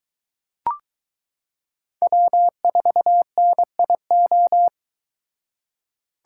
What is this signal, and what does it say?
Morse code, heard as a keyed tone, about 23 wpm, 700 Hz. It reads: W4NIO